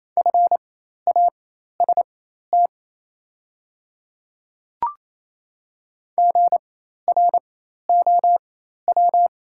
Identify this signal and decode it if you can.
Morse code, heard as a keyed tone, about 28 wpm, spaced as for 13 wpm, 700 Hz. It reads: FAST GROW
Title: 4x Character Spacing - Top 500 Words - 28wpm